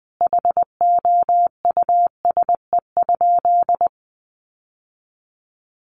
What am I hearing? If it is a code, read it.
Morse code, heard as a keyed tone, about 19 wpm, 700 Hz. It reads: HOUSE?
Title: Sentences from Top 1200 Words 20wpm